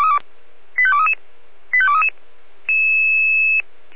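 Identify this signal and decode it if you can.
Five-tone selective call (ZVEI), heard as quick runs of five tones, tones of about 70 ms. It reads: 85233 85233
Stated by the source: ZVEI five-tone selective call 85230 received on an SDR.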